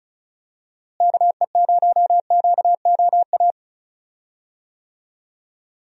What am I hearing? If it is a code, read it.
Morse code, heard as a keyed tone, about 34 wpm, 700 Hz. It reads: KE0QOA
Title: USA Call Signs - 6 Characters - Repeated 3x - 35wpm